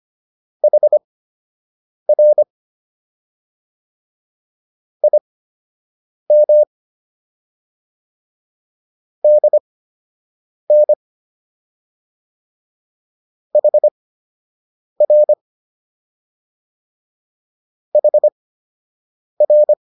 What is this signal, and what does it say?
Morse code, heard as a keyed tone, about 25 wpm, spaced as for 7 wpm, 600 Hz. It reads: HR IM DN HR HR